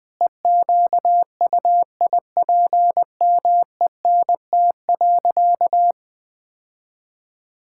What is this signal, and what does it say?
Morse code, heard as a keyed tone, about 21 wpm, 700 Hz. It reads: EQUIPMENT.
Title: Sentences from Top 1000 Words - Review Part 6 - 20wpm